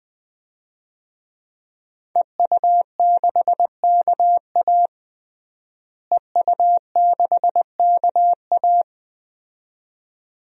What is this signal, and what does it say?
Morse code, heard as a keyed tone, about 19 wpm, 700 Hz. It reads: EU6KA EU6KA
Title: DXCC-100 Call Signs - 5 Characters - Repeated 2x - 20wpm